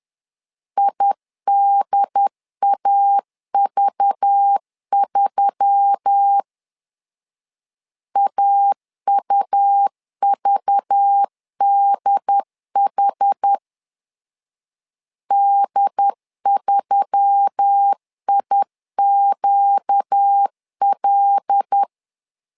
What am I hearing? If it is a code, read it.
Morse code, heard as a keyed tone, about 10 wpm, 800 Hz. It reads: IDAV3 AUVDH D3IQL
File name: LEZIONI / 5 LBVC / 5 LBV 10wpm 2248